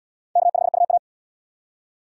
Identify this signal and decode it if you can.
Morse code, heard as a keyed tone, about 71 wpm, 700 Hz. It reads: K5UA